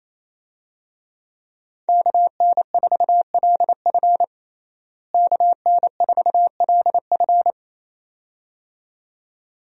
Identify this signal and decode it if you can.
Morse code, heard as a keyed tone, about 27 wpm, 700 Hz. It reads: KN4LF KN4LF
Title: USA Call Signs - 5 Characters - Repeated 2x - 28wpm